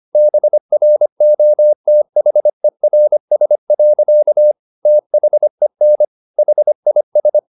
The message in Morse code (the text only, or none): BROTHERS. THEN HIS